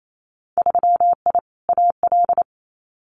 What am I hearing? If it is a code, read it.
Morse code, heard as a keyed tone, about 28 wpm, 690 Hz. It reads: 3I AL
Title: Winter Field Day Exchanges Encore 28wpm